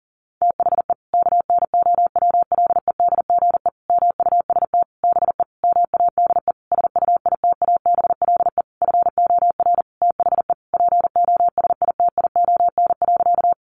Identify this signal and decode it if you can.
Morse code, heard as a keyed tone, about 40 wpm, 700 Hz. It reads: THE KNOWLEDGE MUST BE MADE SUITABLE FOR THE POSITION.
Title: Sentences from Top 1400 Words 40wpm